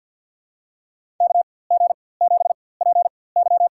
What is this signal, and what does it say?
Morse code, heard as a keyed tone, about 47 wpm, spaced as for 23 wpm, 700 Hz. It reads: KG7PY
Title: USA Call Signs - 4x Character Spacing - 50wpm